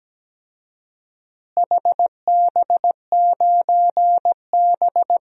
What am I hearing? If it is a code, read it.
Morse code, heard as a keyed tone, about 17 wpm, 700 Hz. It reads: HB9B
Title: DXCC-100 Call Signs - 4 Characters - Repeated 2x - 17wpm